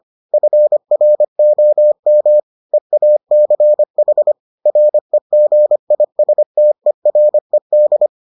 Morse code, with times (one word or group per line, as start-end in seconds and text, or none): 0.3-2.4 FROM
2.7-4.3 EACH
4.7-8.1 REGISTERED